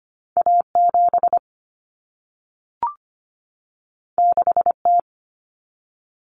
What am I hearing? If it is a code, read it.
Morse code, heard as a keyed tone, about 25 wpm, 700 Hz. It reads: A7 6T
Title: Common Cut Numbers - 2-Digit 25wpm